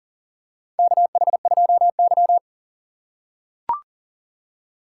Morse code, heard as a keyed tone, about 40 wpm, 700 Hz. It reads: KH2Y
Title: USA Call Signs - 4x Character Spacing - 40wpm